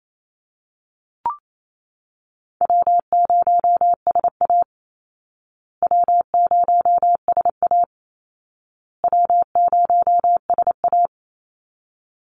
Morse code, heard as a keyed tone, about 28 wpm, 700 Hz. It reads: W0SA W0SA W0SA
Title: Call Signs Repeated 2x - 28wpm